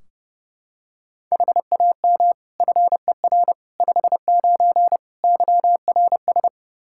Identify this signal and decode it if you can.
Morse code, heard as a keyed tone, about 30 wpm, 700 Hz. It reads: HAM FER 59 YRS
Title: Practice QSO - Exchange 3 - More Info - 30wpm